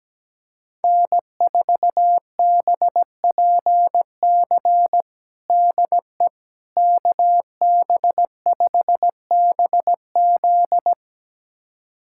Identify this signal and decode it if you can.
Morse code, heard as a keyed tone, about 17 wpm, 700 Hz. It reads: N4BPC DE KB5BZ
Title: USA Callsign DE Callsign - 5 Characters - 17wpm